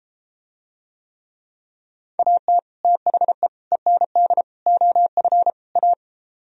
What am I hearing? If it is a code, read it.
Morse code, heard as a keyed tone, about 30 wpm, 700 Hz. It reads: AT THE END OF A